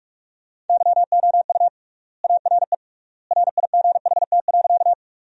Morse code, heard as a keyed tone, about 48 wpm, 700 Hz. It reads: YOU ARE RIGHT.
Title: 3x Word-Spacing - Sentences from Top 200 Words - 45wpm